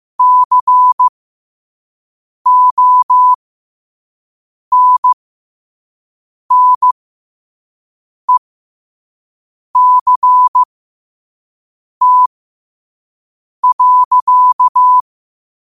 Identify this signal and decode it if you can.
Morse code, heard as a keyed tone, about 15 wpm, spaced as for 5 wpm, 1000 Hz. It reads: CONNECT.
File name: qotdfiles / 5wpm